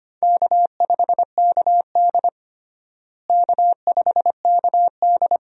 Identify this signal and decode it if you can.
Morse code, heard as a keyed tone, about 25 wpm, 700 Hz. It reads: K5KD K5KD
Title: USA Call Signs - 4 Characters - Repeated 2x - 25wpm